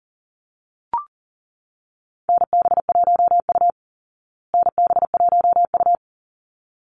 Morse code, heard as a keyed tone, about 40 wpm, 700 Hz. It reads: NB1U NB1U
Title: Call Signs Repeated 3x - 40wpm